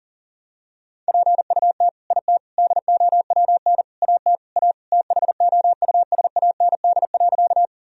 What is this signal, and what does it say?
Morse code, heard as a keyed tone, about 40 wpm, 700 Hz. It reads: PUT IT DOWN AT A THOUSAND.
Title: Sentences from Top 1000 Words - Review Part 3 - 40wpm